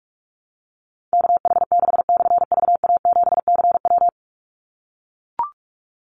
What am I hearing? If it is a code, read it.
Morse code, heard as a keyed tone, about 44 wpm, 700 Hz. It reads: KH6/VA7CW
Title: Call Signs - 45wpm